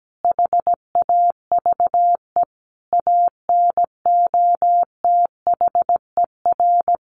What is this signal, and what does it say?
Morse code, heard as a keyed tone, about 17 wpm, 700 Hz. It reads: HAVE ANOTHER
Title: Sentences from Top 600 Words 17wpm